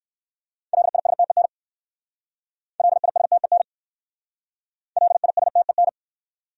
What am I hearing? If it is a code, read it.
Morse code, heard as a keyed tone, about 68 wpm, 700 Hz. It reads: LISTEN LISTEN LISTEN